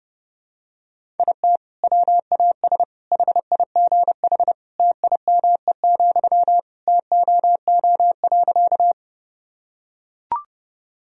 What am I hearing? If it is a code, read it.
Morse code, heard as a keyed tone, about 30 wpm, 700 Hz. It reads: IT WAS HIGH TIME, TOO.
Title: Sentences from Top 1000 Words - Review Part 1 - 30wpm